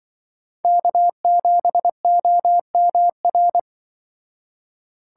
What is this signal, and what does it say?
Morse code, heard as a keyed tone, about 25 wpm, 700 Hz. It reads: K7OMR
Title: Speed Racing - USA 1x3 Call Signs 20wpm